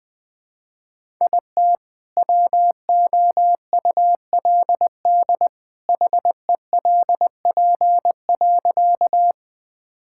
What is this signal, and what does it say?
Morse code, heard as a keyed tone, about 20 wpm, 700 Hz. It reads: IT WOULD HELP.
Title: Sentences with 2-Word Phrases - 20wpm